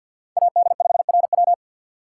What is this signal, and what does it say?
Morse code, heard as a keyed tone, about 49 wpm, 700 Hz. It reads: AD5RW
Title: USA Call Signs - 5 Characters - 4x Character Spacing - 50wpm